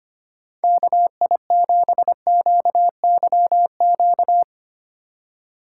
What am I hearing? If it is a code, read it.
Morse code, heard as a keyed tone, about 25 wpm, 700 Hz. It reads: KI7QYQ
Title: USA Call Signs - 6 Characters - 4x Character Spacing - 25wpm